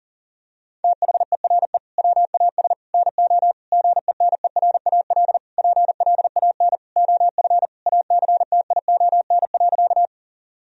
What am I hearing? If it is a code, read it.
Morse code, heard as a keyed tone, about 40 wpm, 700 Hz. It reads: THERE WAS NO GENERAL PLAN OF ACTION.
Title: Sentences from Top 1000 Words - Review Part 4 - 40wpm